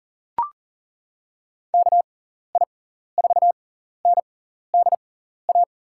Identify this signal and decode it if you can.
Morse code, heard as a keyed tone, about 40 wpm, spaced as for 14 wpm, 700 Hz. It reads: KI4NDA
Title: USA Call Signs - 6 Characters - 6x Character Spacing - 40wpm